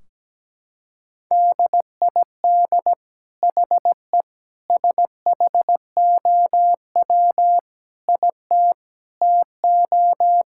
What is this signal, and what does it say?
Morse code, heard as a keyed tone, about 17 wpm, 700 Hz. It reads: DID HE SHOW IT TO
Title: Sentences from Top 1600 Words 17wpm